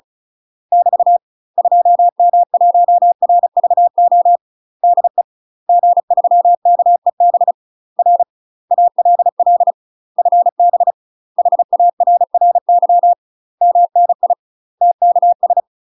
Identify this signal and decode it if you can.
Morse code, heard as a keyed tone, about 36 wpm, 700 Hz. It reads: <BT> 2M1RVO DE G3KEB R ALL FB HARRY MNI TKS
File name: UK MASTER QSOs TRAINING / 35WPM / UK MASTER QSOs 0032